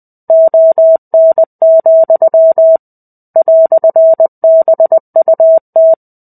Morse code, heard as a keyed tone, about 20 wpm, 650 Hz. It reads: ON, "BUT
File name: 05-Chapter-05 20 WPM